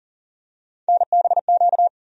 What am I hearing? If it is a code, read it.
Morse code, heard as a keyed tone, about 40 wpm, 700 Hz. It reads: NBQ
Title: USA Call Sign Postfixes - 3 Letters 40wpm